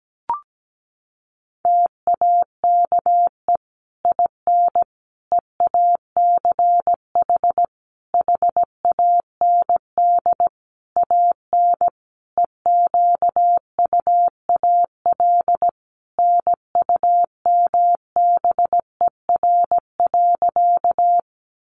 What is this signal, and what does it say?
Morse code, heard as a keyed tone, about 17 wpm, 700 Hz. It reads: TAKE IN EACH HAND AN EQUAL NUMBER.
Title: Sentences from Top 1400 Words 17wpm